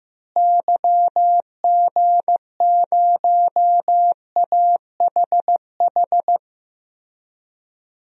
Morse code, heard as a keyed tone, about 15 wpm, 700 Hz. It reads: YG0AHH
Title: DXCC-100 Call Signs - 6 Characters - Repeated 3x - 15wpm